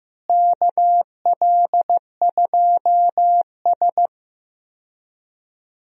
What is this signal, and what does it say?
Morse code, heard as a keyed tone, about 15 wpm, 700 Hz. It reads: KL2S
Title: USA Call Signs - 4 Characters - Repeated 2x - 15wpm